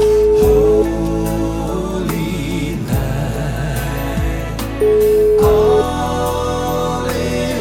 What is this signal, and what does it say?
Ringback tone (Europe, 425 Hz), with noise.